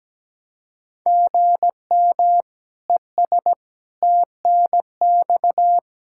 Morse code, heard as a keyed tone, about 17 wpm, 700 Hz. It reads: GM ES TNX
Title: Speed-Racing Common QSO Phrases - 17wpm